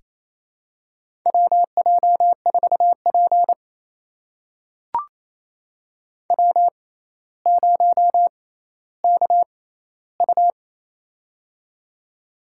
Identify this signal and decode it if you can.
Morse code, heard as a keyed tone, about 27 wpm, spaced as for 14 wpm, 700 Hz. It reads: WJ4P W0KU
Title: USA Call Signs - 6x Character Spacing - 28wpm